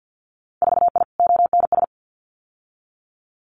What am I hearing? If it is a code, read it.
Morse code, heard as a keyed tone, about 50 wpm, 700 Hz. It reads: 4I ONS
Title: Winter Field Day Exchanges Encore 50wpm